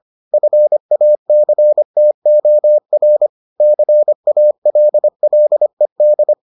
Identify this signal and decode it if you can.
Morse code, heard as a keyed tone, about 25 wpm, 600 Hz. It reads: FACTOR CALLED